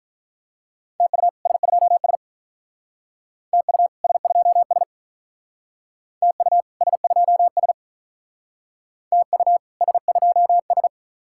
Morse code, heard as a keyed tone, about 44 wpm, 700 Hz. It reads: TU S2S TU S2S TU S2S TU S2S